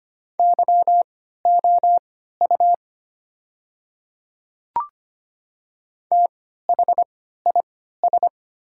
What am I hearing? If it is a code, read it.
Morse code, heard as a keyed tone, about 25 wpm, spaced as for 14 wpm, 700 Hz. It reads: YOU THIS
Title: 3x Character Spacing - Top 100 Words - 25wpm